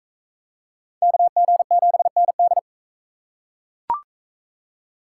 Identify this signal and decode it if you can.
Morse code, heard as a keyed tone, about 39 wpm, 700 Hz. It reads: KG7ND